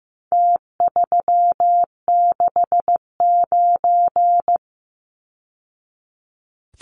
Morse code, heard as a keyed tone, about 15 wpm, 700 Hz. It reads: T369
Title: Numbers - Contest Style - 4-Digit 15wpm